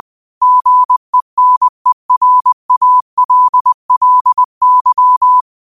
Morse code, heard as a keyed tone, about 20 wpm, 1000 Hz. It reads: GENERALLY
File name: qotdfiles / 20wpm